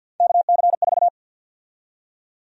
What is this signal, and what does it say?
Morse code, heard as a keyed tone, about 52 wpm, 700 Hz. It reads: KC4
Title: USA Call Sign Prefixes - 50wpm